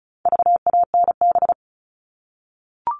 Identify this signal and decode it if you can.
Morse code, heard as a keyed tone, about 34 wpm, 700 Hz. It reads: VANB
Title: All Cut Numbers - 4-Digit 35wpm